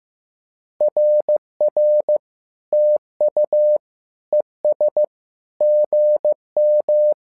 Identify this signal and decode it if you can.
Morse code, heard as a keyed tone, about 15 wpm, 600 Hz. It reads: RR TU ES GM
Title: SOTA QSO Practice 15wpm